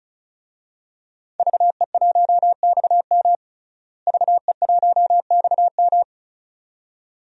Morse code, heard as a keyed tone, about 36 wpm, 700 Hz. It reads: VE1XM VE1XM
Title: DXCC-100 Call Signs - 5 Characters - Repeated 2x - 35wpm